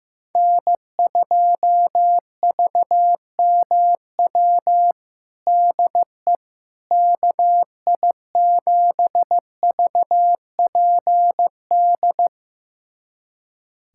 Morse code, heard as a keyed tone, about 16 wpm, 700 Hz. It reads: N2VMW DE KI7VPD
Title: USA Callsign DE Callsign 15wpm